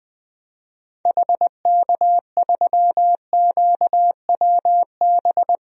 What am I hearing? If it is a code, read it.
Morse code, heard as a keyed tone, about 20 wpm, 700 Hz. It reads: HK3QWB
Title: DXCC-100 Call Signs - 6 Characters - Repeated 2x - 20wpm